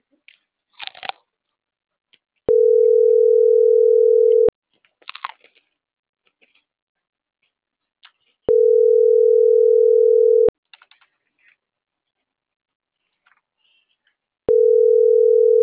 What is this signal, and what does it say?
Ringback tone (North America), 440 + 480 Hz.